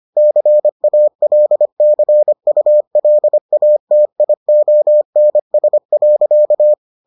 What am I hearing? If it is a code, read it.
Morse code, heard as a keyed tone, about 25 wpm, 600 Hz. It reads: CALCULATIONS.